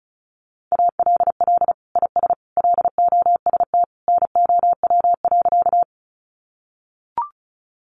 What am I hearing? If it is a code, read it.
Morse code, heard as a keyed tone, about 34 wpm, 700 Hz. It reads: ALL IS LOST NOW.
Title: Sentences from Top 1400 Words 35wpm